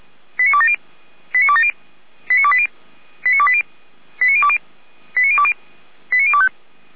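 Five-tone selective call (ZVEI), heard as quick runs of five tones, tones of about 70 ms. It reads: 89270 89270 89280 89280 89020 89020 89025
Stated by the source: Five-tone ZVEI selective call received on SDR.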